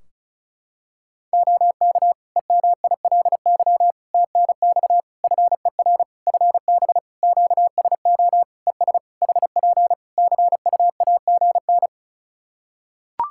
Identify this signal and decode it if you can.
Morse code, heard as a keyed tone, about 34 wpm, 700 Hz. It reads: OK EMILY TNX FER FB QSO ES HP CUAGN
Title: Practice QSO - Exchange 4 - Ending - 35wpm